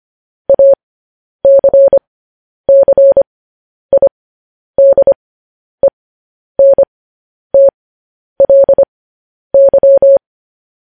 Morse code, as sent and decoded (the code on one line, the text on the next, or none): .- -.-. -.-. .. -.. -. - .-.. -.--
ACCIDENTLY